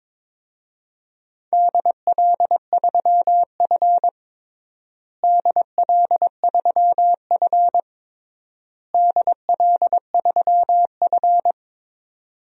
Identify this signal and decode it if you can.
Morse code, heard as a keyed tone, about 21 wpm, 700 Hz. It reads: DL3F DL3F DL3F
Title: DXCC-100 Call Signs - 4 Characters - Repeated 3x - 22wpm